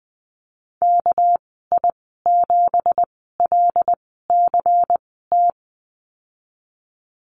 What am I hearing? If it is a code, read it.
Morse code, heard as a keyed tone, about 20 wpm, spaced as for 15 wpm, 700 Hz. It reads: KI7LCT
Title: 219 Lesson 35g - Number 0 - Call Signs 20wpm